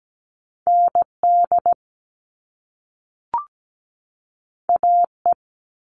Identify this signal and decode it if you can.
Morse code, heard as a keyed tone, about 17 wpm, 700 Hz. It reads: ND AE
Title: Top Two Letter Combinations in Words 17wpm